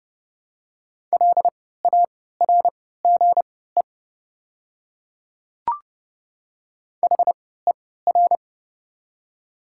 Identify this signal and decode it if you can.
Morse code, heard as a keyed tone, about 30 wpm, spaced as for 17 wpm, 700 Hz. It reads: LARGE HER
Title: 3x Character Spacing - Top 200 Words - 30wpm